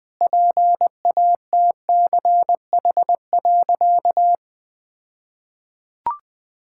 Morse code, heard as a keyed tone, about 20 wpm, 700 Hz. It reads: PATCH.
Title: All-Age Appropriate Jokes - 20wpm